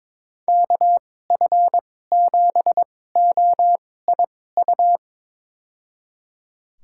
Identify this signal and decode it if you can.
Morse code, heard as a keyed tone, about 22 wpm, spaced as for 16 wpm, 700 Hz. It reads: KF7OIU
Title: USA Call Signs - 6 Characters - 2x Character Spacing - 22wpm